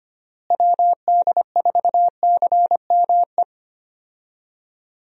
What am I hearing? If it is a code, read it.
Morse code, heard as a keyed tone, about 25 wpm, 700 Hz. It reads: WD4CME
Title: USA Call Signs - 6 Characters - Repeated 3x - 25wpm